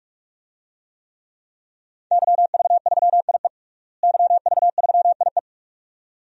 Morse code, heard as a keyed tone, about 42 wpm, 700 Hz. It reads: YV3IE YV3IE
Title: DXCC-100 Call Signs - 5 Characters - Repeated 2x - 45wpm